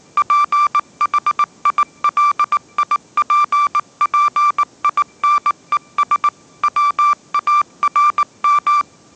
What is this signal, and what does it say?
Morse code, heard as a keyed tone, about 19 wpm, 1180 Hz. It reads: PHILIPPINES WARM
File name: mrx083 20wpm